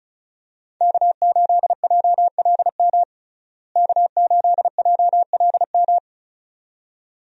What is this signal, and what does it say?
Morse code, heard as a keyed tone, about 36 wpm, 700 Hz. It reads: K8JLM K8JLM
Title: USA Call Signs - 5 Characters - Repeated 2x - 35wpm